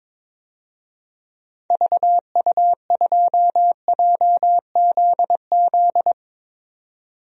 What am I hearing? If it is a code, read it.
Morse code, heard as a keyed tone, about 22 wpm, 700 Hz. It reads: VU2JZZ
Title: DXCC-100 Call Signs - 6 Characters - 22wpm